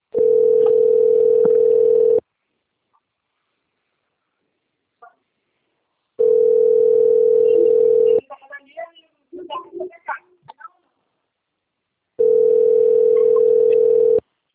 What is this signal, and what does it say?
Ringback tone (North America), 440 + 480 Hz.